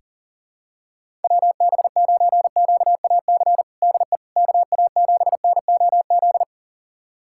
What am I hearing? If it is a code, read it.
Morse code, heard as a keyed tone, about 40 wpm, 700 Hz. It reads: WB9QAC DE KA7NOZ
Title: USA Callsign DE Callsign - 6 Characters - 40wpm